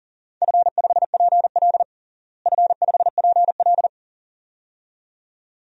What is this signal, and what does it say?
Morse code, heard as a keyed tone, about 40 wpm, 700 Hz. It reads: F5PL F5PL
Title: DXCC-100 Call Signs - 4 Characters - Repeated 2x - 40wpm